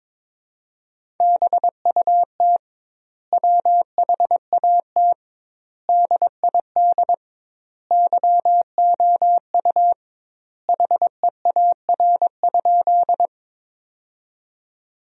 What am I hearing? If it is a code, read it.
Morse code, heard as a keyed tone, about 22 wpm, 700 Hz. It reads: BUT WHAT DID YOU HEAR?
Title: 2x Word-Spacing - Sentences from Top 300 Words - 22wpm